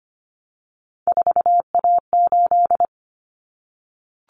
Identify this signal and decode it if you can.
Morse code, heard as a keyed tone, about 25 wpm, 700 Hz. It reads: 4A8
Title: Common Cut Numbers - 3-Digit 25wpm